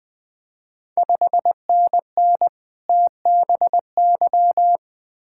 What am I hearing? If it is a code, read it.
Morse code, heard as a keyed tone, about 20 wpm, 700 Hz. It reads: 5NN TBY
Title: Ontario QSO Party Exchange 20wpm